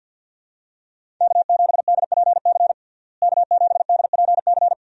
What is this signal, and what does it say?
Morse code, heard as a keyed tone, about 46 wpm, 700 Hz. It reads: K7DPC K7DPC
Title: USA Call Signs - 5 Characters - Repeated 2x - 50wpm